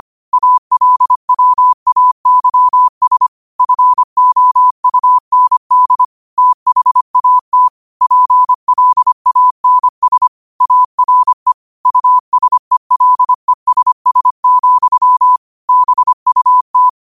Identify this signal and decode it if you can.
Morse code, heard as a keyed tone, about 25 wpm, 1000 Hz. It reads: ALWAYS FOUND THAT PLANS ARE USELESS, BUT
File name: qotdfiles / 25wpm